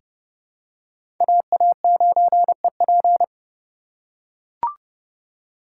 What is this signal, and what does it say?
Morse code, heard as a keyed tone, about 30 wpm, 700 Hz. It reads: AA9EP
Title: USA Call Signs - 5 Characters - 2x Character Spacing - 30wpm